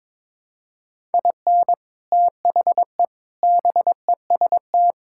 Morse code, heard as a keyed tone, about 22 wpm, 700 Hz. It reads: IN THE BEST
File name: practice / 22wpm / Sets of 6 Words - 22wpm